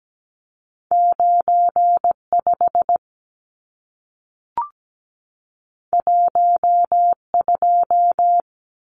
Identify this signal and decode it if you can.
Morse code, heard as a keyed tone, about 17 wpm, 700 Hz. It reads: 95 12
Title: Numbers - 2-Digit 17wpm